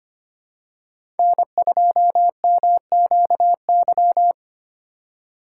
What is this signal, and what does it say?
Morse code, heard as a keyed tone, about 25 wpm, 700 Hz. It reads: N2MQY
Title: USA Call Signs - 5 Characters - 2x Character Spacing - 25wpm